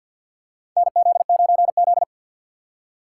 Morse code, heard as a keyed tone, about 52 wpm, 700 Hz. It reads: NZ9Z
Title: USA Call Signs - 4 Characters - 2x Character Spacing - 50wpm